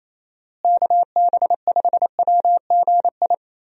Morse code, heard as a keyed tone, about 28 wpm, 700 Hz. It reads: KB5WGI
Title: USA Call Signs - 6 Characters - Repeated 3x - 28wpm